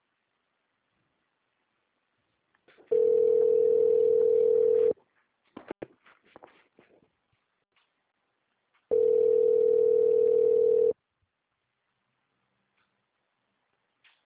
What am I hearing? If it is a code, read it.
Ringback tone (North America), 440 + 480 Hz.
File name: out- -2002- - - .908